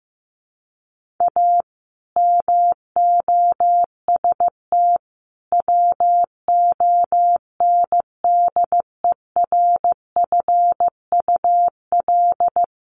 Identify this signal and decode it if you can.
Morse code, heard as a keyed tone, about 15 wpm, 700 Hz. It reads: A MOST WONDERFUL
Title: Sentences from Top 1100 Words 15wpm